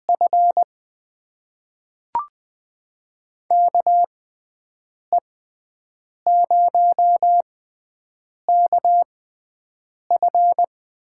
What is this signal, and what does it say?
Morse code, heard as a keyed tone, about 20 wpm, spaced as for 7 wpm, 700 Hz. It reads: F KE0KF
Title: USA Call Signs - 6 Characters - 6x Character Spacing - 20wpm